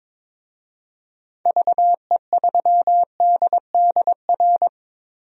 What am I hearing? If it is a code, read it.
Morse code, heard as a keyed tone, about 22 wpm, 700 Hz. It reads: VE3DDR
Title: DXCC-100 Call Signs - 6 Characters - Repeated 3x - 22wpm